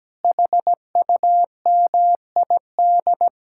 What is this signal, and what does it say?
Morse code, heard as a keyed tone, about 17 wpm, 700 Hz. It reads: HUMID
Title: Speed-Racing Common QSO Phrases - 17wpm